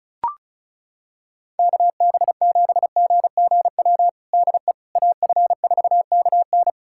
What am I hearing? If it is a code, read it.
Morse code, heard as a keyed tone, about 34 wpm, 700 Hz. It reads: KB7GGW DE AF4KN
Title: USA Callsign DE Callsign 35wpm